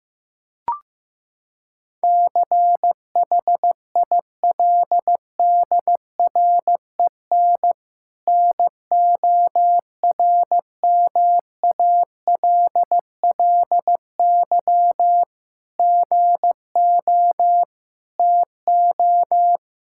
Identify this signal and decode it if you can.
Morse code, heard as a keyed tone, about 15 wpm, 700 Hz. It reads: CHILDREN NORMALLY GO TO
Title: Sentences from Top 1000 Words - Review Part 6 - 15wpm